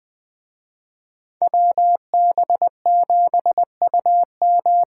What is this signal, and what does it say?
Morse code, heard as a keyed tone, about 21 wpm, 700 Hz. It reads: WB7UM
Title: Speed Racing - USA 2x2 Call Signs 17wpm